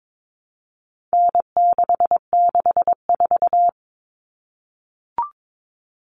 Morse code, heard as a keyed tone, about 22 wpm, 700 Hz. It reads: N664
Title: All Cut Numbers - 4-Digit 22wpm